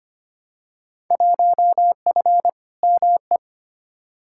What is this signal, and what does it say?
Morse code, heard as a keyed tone, about 25 wpm, 700 Hz. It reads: 1F ME
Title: Field Day Exchanges Encore - 25wpm